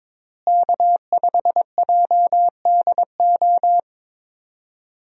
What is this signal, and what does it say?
Morse code, heard as a keyed tone, about 22 wpm, 700 Hz. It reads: K5JDO
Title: USA Call Signs - 5 Characters - Repeated 3x - 22wpm